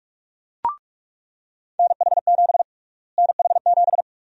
Morse code, heard as a keyed tone, about 44 wpm, 700 Hz. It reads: NH7 NH7
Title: USA Call Sign Prefixes - Repeated 2x - 45wpm